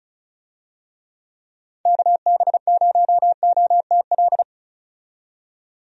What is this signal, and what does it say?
Morse code, heard as a keyed tone, about 36 wpm, 700 Hz. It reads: KB0OTL
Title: USA Call Signs - 6 Characters - 35wpm